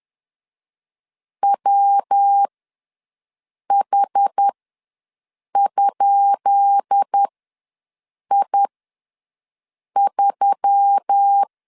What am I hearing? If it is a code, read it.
Morse code, heard as a keyed tone, about 10 wpm, spaced as for 5 wpm, 800 Hz. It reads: WH?I3
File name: LEZIONI / 10 SIGNS / 10D COMA ws 10wpm 4314